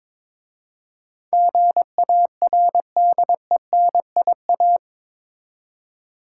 Morse code, heard as a keyed tone, about 22 wpm, 700 Hz. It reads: GARDENIA
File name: Difficult Words with Hints - 22wpm